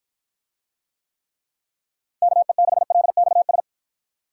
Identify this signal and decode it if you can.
Morse code, heard as a keyed tone, about 52 wpm, 700 Hz. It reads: KE6LXS